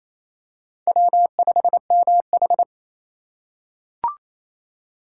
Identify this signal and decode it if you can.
Morse code, heard as a keyed tone, about 28 wpm, 700 Hz. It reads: W5MH
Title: USA Call Signs - 4 Characters - 3x Character Spacing - 28wpm